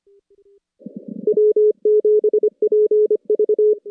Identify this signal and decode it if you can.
Morse code, heard as a keyed tone, about 27 wpm, 410 Hz. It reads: W7PV
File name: contests / CW-SS-k5tr- -2100